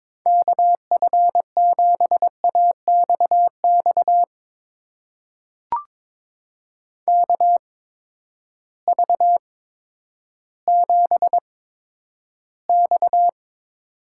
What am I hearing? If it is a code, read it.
Morse code, heard as a keyed tone, about 22 wpm, spaced as for 11 wpm, 700 Hz. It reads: KF7AXX KV7X
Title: USA Call Signs - 8x Character Spacing - 22wpm